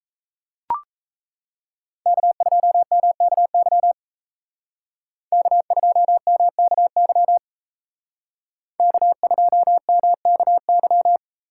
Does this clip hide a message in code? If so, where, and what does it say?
Yes: Morse code from 2.1 to 11.2 s, heard as a keyed tone, about 36 wpm, 700 Hz. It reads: K2MKY K2MKY K2MKY